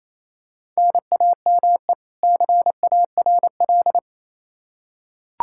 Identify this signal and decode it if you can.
Morse code, heard as a keyed tone, about 28 wpm, 700 Hz. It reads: NAME CARL
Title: Practice QSO - Exchange 1 - Beginning - 28wpm